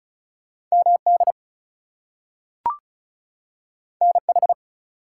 Morse code, heard as a keyed tone, about 36 wpm, 700 Hz. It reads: MD NH
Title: US States and Canadian Provinces 35wpm